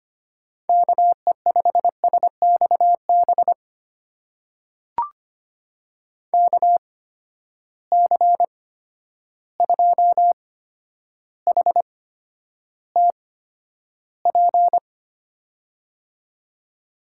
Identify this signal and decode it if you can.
Morse code, heard as a keyed tone, about 25 wpm, spaced as for 11 wpm, 700 Hz. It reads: KE5SXB KC2HTP
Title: USA Call Signs - 6 Characters - 8x Character Spacing - 25wpm